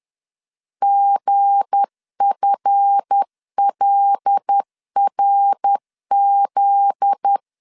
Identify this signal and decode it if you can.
Morse code, heard as a keyed tone, about 10 wpm, 800 Hz. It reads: GFLRZ
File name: LEZIONI / 7 GF9Z / 7 GF9Z 10wpm 2611